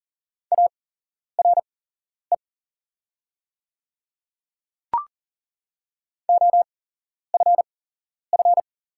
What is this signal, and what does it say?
Morse code, heard as a keyed tone, about 41 wpm, spaced as for 11 wpm, 700 Hz. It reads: ARE OFF
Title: 8x Character Spacing - Top 300 Words - 40wpm